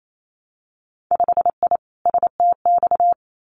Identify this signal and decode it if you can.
Morse code, heard as a keyed tone, about 28 wpm, 700 Hz. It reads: 5I STX
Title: Winter Field Day Exchanges Encore 28wpm